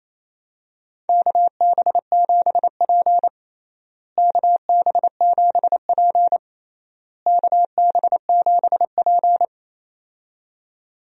Morse code, heard as a keyed tone, about 28 wpm, 700 Hz. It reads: KB7P KB7P KB7P
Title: USA Call Signs - 4 Characters - Repeated 3x - 28wpm